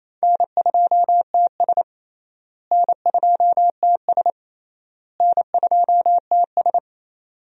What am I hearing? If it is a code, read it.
Morse code, heard as a keyed tone, about 28 wpm, 700 Hz. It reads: N2TS N2TS N2TS
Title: USA Call Signs - 4 Characters - Repeated 3x - 28wpm